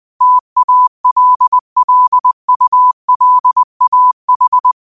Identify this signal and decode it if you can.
Morse code, heard as a keyed tone, about 20 wpm, 1000 Hz. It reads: TALLULAH
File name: qotdfiles / 20wpm